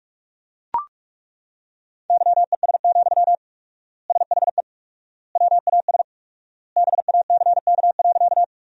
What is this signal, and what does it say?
Morse code, heard as a keyed tone, about 42 wpm, 700 Hz. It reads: YES, SHE WAS BACK.
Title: 4x Word-Spacing - Sentences from Top 500 Words - 45wpm